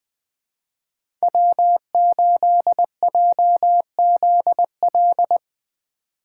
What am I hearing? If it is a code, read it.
Morse code, heard as a keyed tone, about 23 wpm, 700 Hz. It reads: W8JZL
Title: Speed Racing - USA 1x3 Call Signs 20wpm